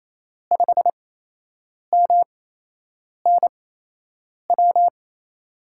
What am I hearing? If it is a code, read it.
Morse code, heard as a keyed tone, about 28 wpm, spaced as for 8 wpm, 700 Hz. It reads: 5MNW
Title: USA Call Signs - 6 Characters - 8x Character Spacing - 28wpm